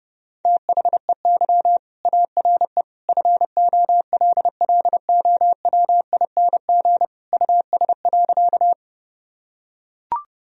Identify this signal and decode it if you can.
Morse code, heard as a keyed tone, about 30 wpm, 700 Hz. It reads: THEY ARE FOLLOWING US.
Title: Sentences from Top 1000 Words - Review Part 4 - 30wpm